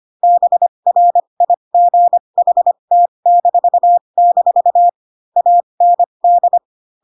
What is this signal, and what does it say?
Morse code, heard as a keyed tone, about 25 wpm, spaced as for 22 wpm, 700 Hz. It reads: BRIGHT-- AND
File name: Pinocchio / Pinocchio ch7 18 25 ik1hge